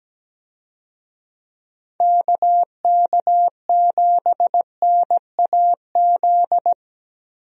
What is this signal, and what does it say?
Morse code, heard as a keyed tone, about 17 wpm, 700 Hz. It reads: KK7NAZ
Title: USA Call Signs - 6 Characters - Repeated 3x - 17wpm